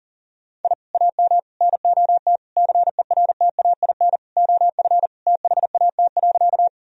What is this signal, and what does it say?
Morse code, heard as a keyed tone, about 40 wpm, 700 Hz. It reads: I AM NOT CERTAIN OF THAT.
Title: Sentences from Top 1000 Words - Review Part 2 - 40wpm